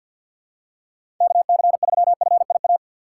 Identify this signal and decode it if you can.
Morse code, heard as a keyed tone, about 48 wpm, 700 Hz. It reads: KC3FIA
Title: USA Call Signs - 6 Characters - 6x Character Spacing - 50wpm